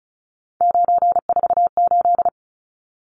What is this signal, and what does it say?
Morse code, heard as a keyed tone, about 34 wpm, 700 Hz. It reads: 948
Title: Numbers - 3-Digit 35wpm